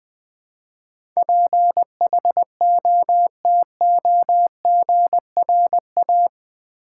Morse code, heard as a keyed tone, about 19 wpm, 700 Hz. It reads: PHOTOGRA
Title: Build A Word - 4 Letters At A Time - 12-Letter Words - 20wpm